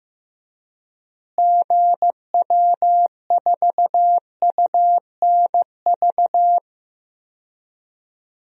Morse code, heard as a keyed tone, about 15 wpm, 700 Hz. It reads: GW4UNV
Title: DXCC-100 Call Signs - 6 Characters - Repeated 3x - 15wpm